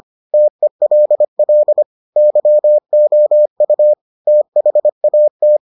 Morse code, heard as a keyed tone, about 25 wpm, 600 Hz. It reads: TELL YOU THAT